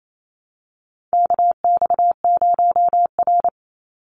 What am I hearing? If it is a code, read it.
Morse code, heard as a keyed tone, about 28 wpm, 700 Hz. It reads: KX0R
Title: NA SOTA Activator Callsigns 28wpm